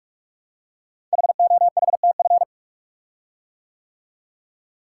Morse code, heard as a keyed tone, about 44 wpm, 690 Hz. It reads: HOHTF
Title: Code Groups - 5 Letters - 45wpm